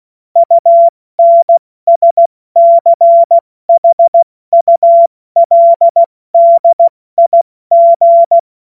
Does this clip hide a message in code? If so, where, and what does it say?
Yes: Morse code from 0.4 to 8.4 s, heard as a keyed tone, about 16 wpm, spaced as for 14 wpm, 680 Hz. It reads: UNSCHULDIG